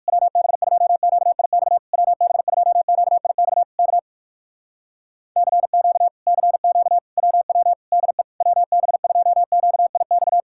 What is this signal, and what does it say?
Morse code, heard as a keyed tone, about 68 wpm, spaced as for 58 wpm, 700 Hz. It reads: WB2QIX WB2QIX K CQ CQ WW DE WB2QIX